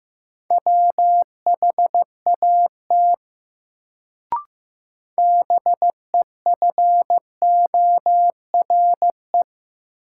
Morse code, heard as a keyed tone, about 15 wpm, 700 Hz. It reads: WHAT BEFORE
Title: Sets of 2 Words - New Word Drill - 15wpm